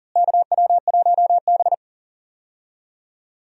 Morse code, heard as a keyed tone, about 37 wpm, 700 Hz. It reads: KW1B
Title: USA Call Signs - 4 Characters - Repeated 3x - 40wpm